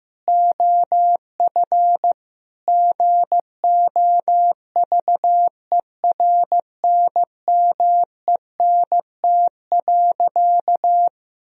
Morse code, heard as a keyed tone, about 16 wpm, 700 Hz. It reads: OF GOVERNMENT.
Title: Sentences from Top 1000 Words - Review Part 3 - 15wpm